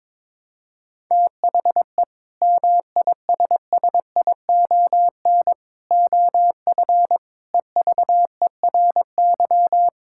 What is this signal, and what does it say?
Morse code, heard as a keyed tone, about 22 wpm, 700 Hz. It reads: THE MISSION OF EVERY
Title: Sentences from Top 1600 Words 22wpm